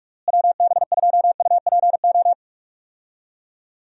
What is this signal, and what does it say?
Morse code, heard as a keyed tone, about 42 wpm, 700 Hz. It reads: WB2UPO
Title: USA Call Signs - 6 Characters - Repeated 3x - 45wpm